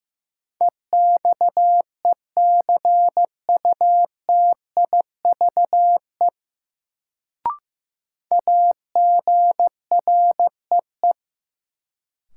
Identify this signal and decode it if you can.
Morse code, heard as a keyed tone, about 15 wpm, 700 Hz. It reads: EXECUTIVE AGREE
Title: Sets of 5 Words Encore - New Word Drill - 15wpm